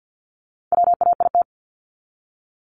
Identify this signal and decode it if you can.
Morse code, heard as a keyed tone, about 50 wpm, 700 Hz. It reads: WAIT
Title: ING Suffix - Top 100 Words with Base Words 50wpm